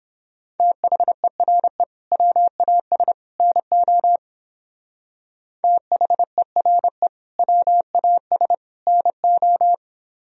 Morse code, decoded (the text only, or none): THERE WAS NO THERE WAS NO